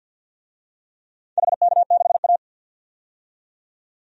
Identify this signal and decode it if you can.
Morse code, heard as a keyed tone, about 51 wpm, 700 Hz. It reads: HK6A
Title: DXCC-100 Call Signs - 4 Characters - 50wpm